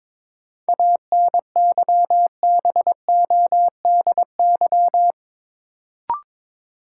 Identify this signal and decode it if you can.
Morse code, heard as a keyed tone, about 22 wpm, 700 Hz. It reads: ANYBODY
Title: Sets of 5 Words - New Word Drill - 22wpm